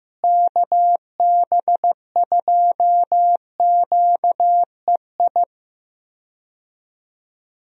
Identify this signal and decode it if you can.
Morse code, heard as a keyed tone, about 15 wpm, 700 Hz. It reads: KB2QEI
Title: USA Call Signs - 6 Characters - Repeated 2x - 15wpm